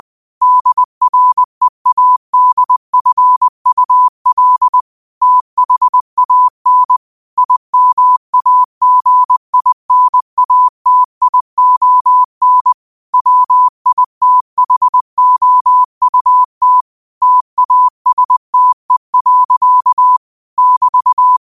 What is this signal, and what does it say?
Morse code, heard as a keyed tone, about 20 wpm, 1000 Hz. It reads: DREADFUL THAN IMAGINATION WITHOUT TASTE. <BT>
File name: qotdfiles / 20wpm